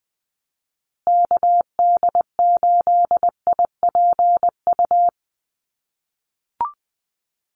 Morse code, heard as a keyed tone, about 20 wpm, 700 Hz. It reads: KD8IPU
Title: USA Call Signs Repeated 2x 20wpm